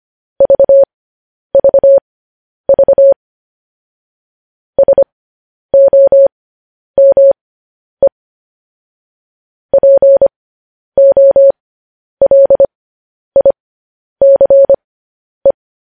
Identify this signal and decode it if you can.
Morse code, heard as a keyed tone, about 25 wpm, spaced as for 10 wpm, 550 Hz. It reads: VVV SOME POLICE
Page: news headlines in morse code at 10 wpm